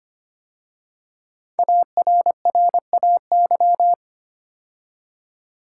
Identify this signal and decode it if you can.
Morse code, heard as a keyed tone, about 25 wpm, 700 Hz. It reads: ARRAY
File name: 5 Letter Words - First 3 Letters Given - 25wpm